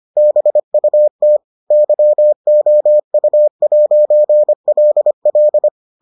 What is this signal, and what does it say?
Morse code, heard as a keyed tone, about 25 wpm, 600 Hz. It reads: BUT YOU'LL